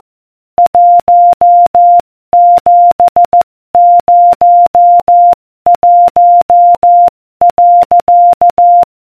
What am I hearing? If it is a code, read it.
Morse code, heard as a keyed tone, about 14 wpm, 700 Hz. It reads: 1701.